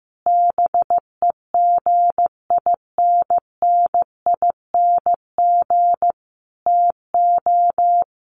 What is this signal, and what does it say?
Morse code, heard as a keyed tone, about 15 wpm, 700 Hz. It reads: BEGINNING TO
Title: Sentences from Top 1500 Words 15wpm